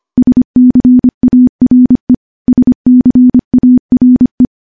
Morse code, heard as a keyed tone, about 25 wpm, 250 Hz. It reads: SCARE SCARE
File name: morse-training-25WPM